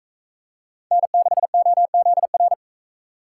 Morse code, heard as a keyed tone, about 40 wpm, 700 Hz. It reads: N6OZR